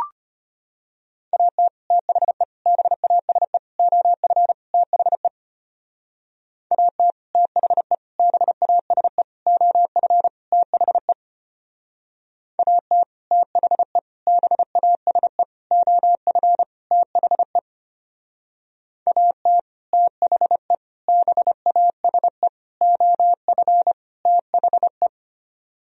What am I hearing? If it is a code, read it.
Morse code, heard as a keyed tone, about 33 wpm, 700 Hz. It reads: AT THE BASE OF THE AT THE BASE OF THE AT THE BASE OF THE AT THE BASE OF THE